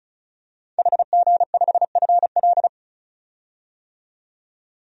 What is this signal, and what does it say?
Morse code, heard as a keyed tone, about 34 wpm, 700 Hz. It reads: HG5FL
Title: DXCC-100 Call Signs - 5 Characters - 35wpm